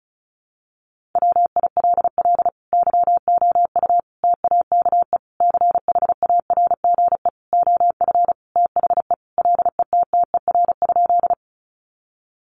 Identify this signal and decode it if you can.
Morse code, heard as a keyed tone, about 34 wpm, 700 Hz. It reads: WILL YOU TAKE CHARGE OF THE LETTER?
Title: Sentences from Top 900 Words 35wpm